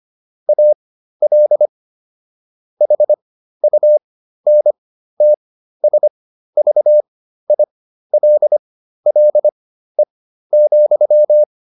Morse code, heard as a keyed tone, about 25 wpm, spaced as for 13 wpm, 600 Hz. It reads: AL HUNTSVILLE,